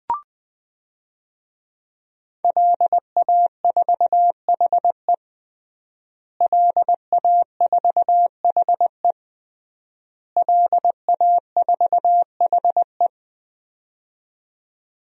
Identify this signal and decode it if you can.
Morse code, heard as a keyed tone, about 20 wpm, 700 Hz. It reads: LA4HE LA4HE LA4HE
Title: DXCC-100 Call Signs - 5 Characters - Repeated 3x - 20wpm